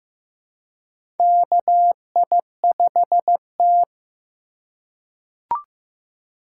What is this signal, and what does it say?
Morse code, heard as a keyed tone, about 15 wpm, 700 Hz. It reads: KI5T
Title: USA Call Signs - 4 Characters - 2x Character Spacing - 15wpm